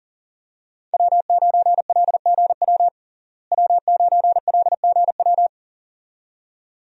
Morse code, heard as a keyed tone, about 40 wpm, 700 Hz. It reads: W9LGW W9LGW
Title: USA Call Signs - 5 Characters - Repeated 2x - 40wpm